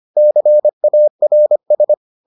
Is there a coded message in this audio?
Yes: Morse code from 0.2 to 1.9 s, heard as a keyed tone, about 25 wpm, 600 Hz. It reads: CARS